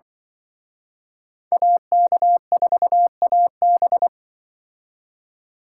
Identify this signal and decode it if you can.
Morse code, heard as a keyed tone, about 25 wpm, 700 Hz. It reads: AK4AB
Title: Speed Racing - USA Call Signs 20wpm